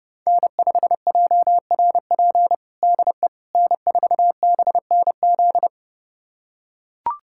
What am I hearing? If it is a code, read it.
Morse code, heard as a keyed tone, about 32 wpm, 700 Hz. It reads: N5JRP DE N4BNZ
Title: USA Callsign DE Callsign - 5 Characters - 30wpm